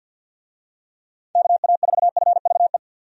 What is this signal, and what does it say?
Morse code, heard as a keyed tone, about 46 wpm, 700 Hz. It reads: KA4FVE
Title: USA Call Signs - 6 Characters - 50wpm